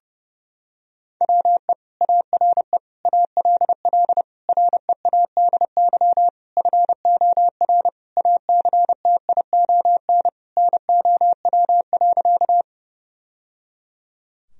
Morse code, heard as a keyed tone, about 32 wpm, 700 Hz. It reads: WE ARE ALL READY FOR ACTION NOW.
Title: Sentences from Top 1000 Words - Review Part 4 - 30wpm